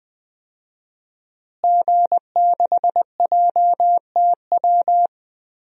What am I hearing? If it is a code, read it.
Morse code, heard as a keyed tone, about 20 wpm, 700 Hz. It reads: G6JTW
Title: DXCC-100 Call Signs - 5 Characters - Repeated 3x - 20wpm